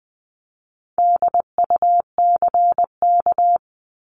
Morse code, heard as a keyed tone, about 20 wpm, 700 Hz. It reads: DUCK
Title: 260 Lesson 42e - Words 20-20wpm